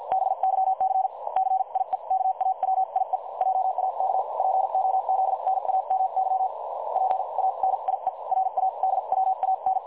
Morse code, heard as a keyed tone, about 50 wpm, spaced as for 42 wpm, 760 Hz. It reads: BYC CIEKAWIE JAK POZNAMY PANIE RADWAN
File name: dlazaawansowanych / nagraniazpasma / websdr recording 3565.0kHz